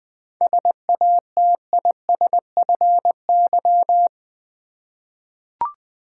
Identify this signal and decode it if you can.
Morse code, heard as a keyed tone, about 20 wpm, 690 Hz. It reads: SATISFY
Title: 7 Letter Words - First 3 Letters Given - 20wpm